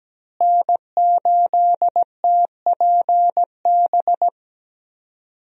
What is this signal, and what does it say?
Morse code, heard as a keyed tone, about 17 wpm, 700 Hz. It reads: N8TPB
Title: USA Call Signs - 5 Characters - Repeated 3x - 17wpm